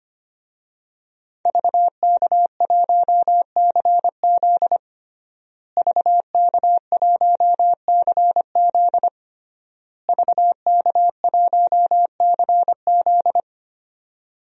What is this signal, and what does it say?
Morse code, heard as a keyed tone, about 25 wpm, 700 Hz. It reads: VK1CZ VK1CZ VK1CZ
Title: DXCC-100 Call Signs - 5 Characters - Repeated 3x - 25wpm